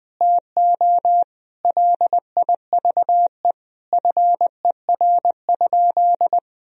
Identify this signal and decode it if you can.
Morse code, heard as a keyed tone, about 20 wpm, 700 Hz. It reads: TO LIVE FER?
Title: Sentences from Top 200 Words and Abbreviations 20wpm